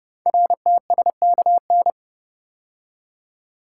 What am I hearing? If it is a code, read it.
Morse code, heard as a keyed tone, about 29 wpm, 700 Hz. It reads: RTSKN
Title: Code Groups - 5 Letters - 30wpm